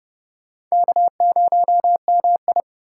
Morse code, heard as a keyed tone, about 39 wpm, 700 Hz. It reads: K0MI